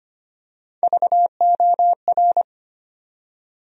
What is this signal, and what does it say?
Morse code, heard as a keyed tone, about 25 wpm, 700 Hz. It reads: VOR
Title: USA Call Sign Postfixes - 3 Letters 25wpm